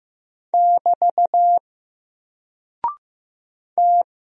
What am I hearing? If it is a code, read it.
Morse code, heard as a keyed tone, about 15 wpm, 700 Hz. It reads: <BT> T
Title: Single Character (Letter:Number:Punctuation:Prosign) 15wpm